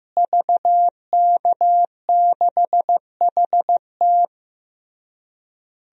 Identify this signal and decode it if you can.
Morse code, heard as a keyed tone, about 15 wpm, 700 Hz. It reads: VK6HT
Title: DXCC-100 Call Signs - 5 Characters - Repeated 3x - 15wpm